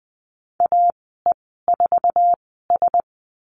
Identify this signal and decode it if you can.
Morse code, heard as a keyed tone, about 20 wpm, spaced as for 14 wpm, 700 Hz. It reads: AE4S
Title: 069 Lesson 12g - Letter d - Call Signs 20wpm